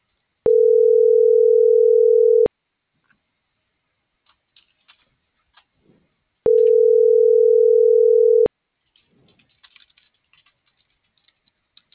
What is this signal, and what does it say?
Ringback tone (North America), 440 + 480 Hz.